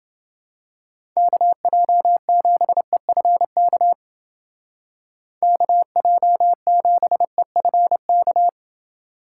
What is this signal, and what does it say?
Morse code, heard as a keyed tone, about 29 wpm, 700 Hz. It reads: KJ7EFK KJ7EFK